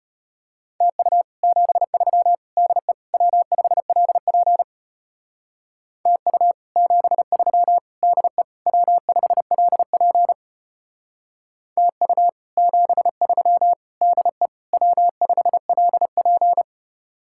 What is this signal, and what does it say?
Morse code, heard as a keyed tone, about 32 wpm, 700 Hz. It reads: TU 73 DE W5LP TU 73 DE W5LP TU 73 DE W5LP